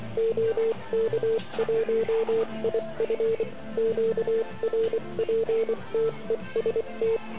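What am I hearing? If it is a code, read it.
Morse code, heard as a keyed tone, about 23 wpm, 440 Hz, with noise. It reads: OK1IF QRPTEST
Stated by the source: Morse code from the OK QRP zavod 2018 contest, Chrudim.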